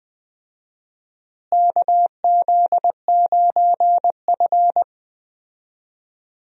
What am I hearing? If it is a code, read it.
Morse code, heard as a keyed tone, about 20 wpm, 700 Hz. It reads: KZ9F
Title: USA Call Signs - 4 Characters - Repeated 2x - 20wpm